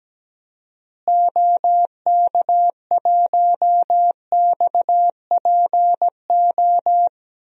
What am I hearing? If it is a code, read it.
Morse code, heard as a keyed tone, about 17 wpm, 700 Hz. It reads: OK1XPO
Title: DXCC-100 Call Signs - 6 Characters - Repeated 2x - 17wpm